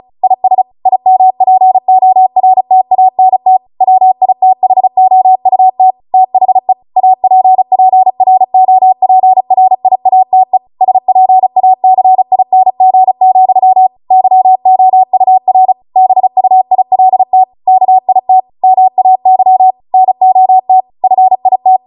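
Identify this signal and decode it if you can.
Morse code, heard as a keyed tone, about 36 wpm, 750 Hz. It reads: IS IMPORTANT WITHOUT THE APPROPRIATE SPACING, YOUR BUILT KIT MAY NOT FIT
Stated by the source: Amateur radio Morse code practice archive transmission at 35 wpm.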